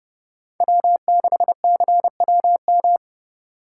Morse code, heard as a keyed tone, about 32 wpm, 700 Hz. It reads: W6CWM
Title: K1USN SST QSO Practice - 30wpm